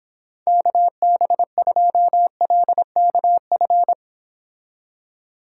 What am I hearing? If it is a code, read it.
Morse code, heard as a keyed tone, about 25 wpm, 700 Hz. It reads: KB2LKF